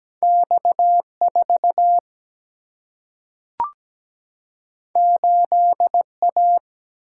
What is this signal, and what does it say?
Morse code, heard as a keyed tone, about 17 wpm, 700 Hz. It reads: X4 8A
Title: Single-Letter Single-Number 17wpm